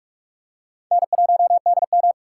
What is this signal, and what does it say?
Morse code, heard as a keyed tone, about 44 wpm, 700 Hz. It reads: N1DM